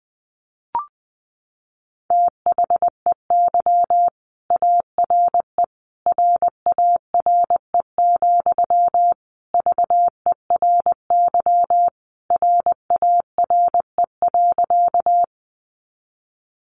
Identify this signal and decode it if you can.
Morse code, heard as a keyed tone, about 20 wpm, 700 Hz. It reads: THEY ARE RARE, VERY RARE.
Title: Sentences from Top 1100 Words 20wpm